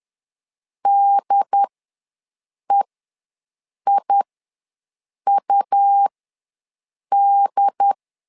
Morse code, heard as a keyed tone, about 10 wpm, spaced as for 6 wpm, 800 Hz. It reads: DEIUD
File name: LEZIONI / 2 UHDN / 2 UHD ws 10wpm 2058